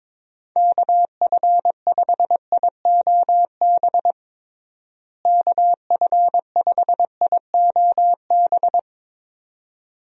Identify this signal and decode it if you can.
Morse code, heard as a keyed tone, about 22 wpm, 700 Hz. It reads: KF5IOB KF5IOB
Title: USA Call Signs - 6 Characters - Repeated 2x - 22wpm